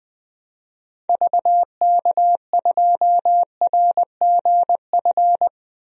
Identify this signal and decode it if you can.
Morse code, heard as a keyed tone, about 20 wpm, 700 Hz. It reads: VK2RGF
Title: DXCC-100 Call Signs - 6 Characters - 20wpm